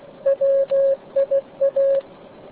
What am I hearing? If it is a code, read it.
Morse code, heard as a keyed tone, about 16 wpm, 550 Hz. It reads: WIA